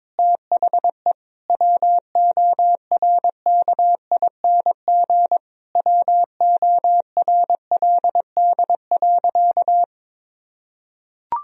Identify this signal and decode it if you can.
Morse code, heard as a keyed tone, about 21 wpm, 700 Hz. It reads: THE WORKING WORLD.
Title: Sentences from Top 1000 Words - Review Part 4 - 22wpm